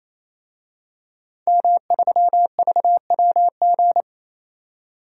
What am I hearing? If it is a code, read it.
Morse code, heard as a keyed tone, about 28 wpm, 700 Hz. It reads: M3VWG
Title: DXCC-100 Call Signs - 5 Characters - 28wpm